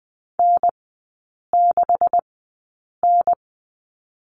Morse code, heard as a keyed tone, about 20 wpm, spaced as for 8 wpm, 700 Hz. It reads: N6N
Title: 127 Lesson 21e - Number 6 - Three Characters 20wpm